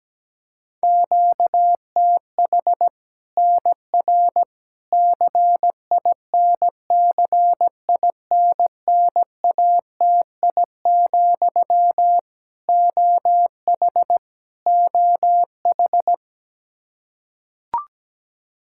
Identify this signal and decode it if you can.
Morse code, heard as a keyed tone, about 17 wpm, 700 Hz. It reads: QTH NR CINCINNATI, OH OH
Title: Practice QSO - Exchange 1 - Beginning - 17wpm